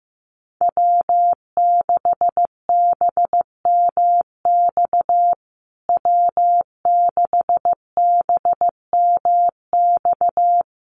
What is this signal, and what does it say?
Morse code, heard as a keyed tone, about 14 wpm, 700 Hz. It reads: W6BMX W6BMX
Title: USA Call Signs Repeated 3x 15wpm